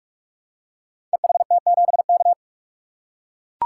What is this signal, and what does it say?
Morse code, heard as a keyed tone, about 44 wpm, 700 Hz. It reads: EHT7K
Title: Code Groups - 5 Letter:Numbers - 45wpm